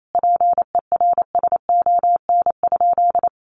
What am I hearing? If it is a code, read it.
Morse code, heard as a keyed tone, about 28 wpm, 700 Hz. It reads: PERSON?
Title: Sentences from Top 1400 Words 28wpm